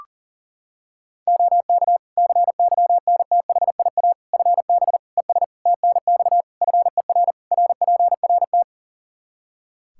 Morse code, heard as a keyed tone, about 40 wpm, 700 Hz. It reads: OK CYNTHIA FB ES TNX FER RPRT
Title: K1USN SST QSO Practice 40wpm